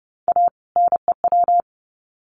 Morse code, heard as a keyed tone, about 29 wpm, 700 Hz. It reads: A NEW
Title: Sets of 2 Words 30wpm